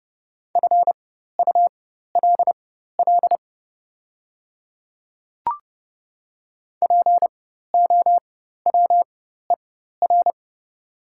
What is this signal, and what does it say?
Morse code, heard as a keyed tone, about 30 wpm, spaced as for 14 wpm, 690 Hz. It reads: FULL POWER
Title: 4x Character Spacing - Top 500 Words - 30wpm